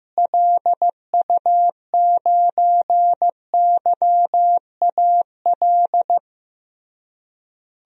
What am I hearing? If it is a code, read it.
Morse code, heard as a keyed tone, about 14 wpm, 700 Hz. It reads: LU9YAL
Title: DXCC-100 Call Signs - 6 Characters - Repeated 3x - 15wpm